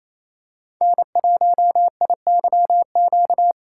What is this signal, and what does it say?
Morse code, heard as a keyed tone, about 28 wpm, 700 Hz. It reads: N1IYQ
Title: USA Call Signs - 5 Characters - 4x Character Spacing - 28wpm